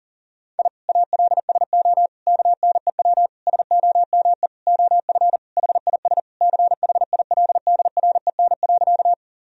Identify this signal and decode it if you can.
Morse code, heard as a keyed tone, about 40 wpm, 700 Hz. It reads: I ALSO KNEW SOME OF HIS CHILDREN.
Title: Sentences from Top 1000 Words - Review Part 1 - 40wpm